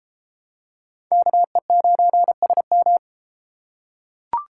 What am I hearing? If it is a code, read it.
Morse code, heard as a keyed tone, about 31 wpm, 700 Hz. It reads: KE9SM